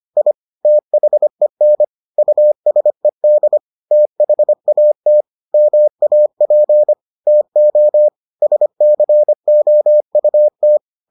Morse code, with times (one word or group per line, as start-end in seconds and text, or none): 0.2-0.3 I
0.6-1.8 THEN
2.2-3.6 USED
3.9-5.2 THAT
5.5-6.9 MAP
7.3-8.1 TO
8.4-10.8 SCOUT